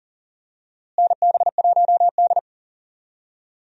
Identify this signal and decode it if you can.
Morse code, heard as a keyed tone, about 40 wpm, 700 Hz. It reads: NB1D
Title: USA Call Signs - 4 Characters - Repeated 3x - 40wpm